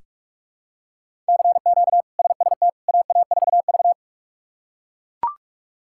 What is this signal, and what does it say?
Morse code, heard as a keyed tone, about 42 wpm, 700 Hz. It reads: CQ SST AA4V
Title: K1USN SST QSO Practice - 45wpm